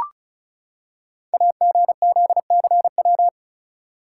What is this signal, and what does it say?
Morse code, heard as a keyed tone, about 36 wpm, 690 Hz. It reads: AGZCW
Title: Code Groups - 5 Letters - 35wpm